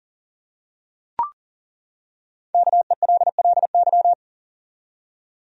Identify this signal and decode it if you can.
Morse code, heard as a keyed tone, about 40 wpm, 700 Hz. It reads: KELLY
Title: USA Common Names - 40wpm